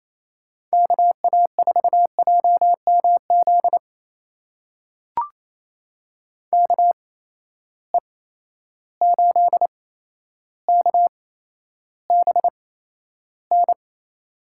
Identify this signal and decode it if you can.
Morse code, heard as a keyed tone, about 28 wpm, spaced as for 12 wpm, 700 Hz. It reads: KA4JMZ KE8KBN
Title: USA Call Signs - 6 Characters - 8x Character Spacing - 28wpm